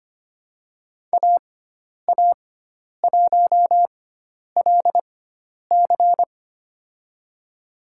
Morse code, heard as a keyed tone, about 25 wpm, spaced as for 10 wpm, 700 Hz. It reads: AA1LC
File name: Lesson 16g - Letter c - Call Signs 25wpm